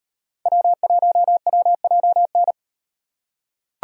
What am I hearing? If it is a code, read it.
Morse code, heard as a keyed tone, about 38 wpm, 700 Hz. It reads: W1WJN